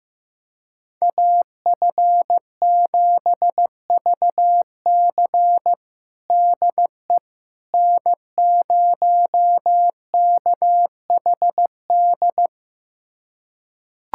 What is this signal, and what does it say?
Morse code, heard as a keyed tone, about 15 wpm, 700 Hz. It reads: AF7VC DE N0KHD
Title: USA Callsign DE Callsign - 5 Characters - 15wpm